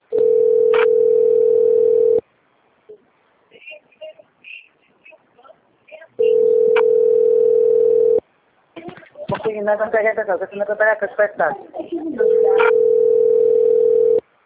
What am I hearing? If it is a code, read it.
Ringback tone (North America), 440 + 480 Hz, with speech.
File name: rg-1600-unknown- - - .55862